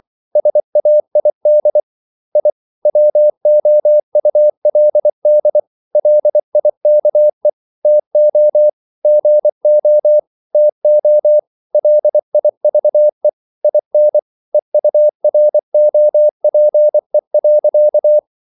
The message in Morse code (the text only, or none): SAID I WOULD LIKE TO GO TO LIVE IN EUROPE.